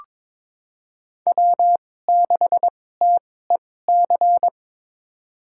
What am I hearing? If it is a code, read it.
Morse code, heard as a keyed tone, about 22 wpm, spaced as for 16 wpm, 700 Hz. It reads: W6TEC
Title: USA Call Signs - 5 Characters - 2x Character Spacing - 22wpm